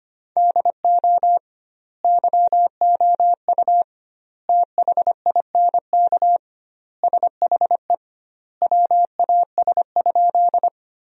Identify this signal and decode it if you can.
Morse code, heard as a keyed tone, about 25 wpm, 700 Hz. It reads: DO YOU THINK SHE WAS?
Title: 2x Word-Spacing - Sentences from Top 200 Words - 25wpm